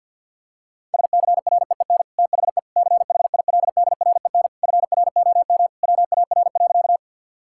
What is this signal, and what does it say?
Morse code, heard as a keyed tone, about 50 wpm, 700 Hz. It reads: SCREEN THE CHILDREN FROM WAR.
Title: Sentences from Top 1000 Words - Review Part 6 - 50wpm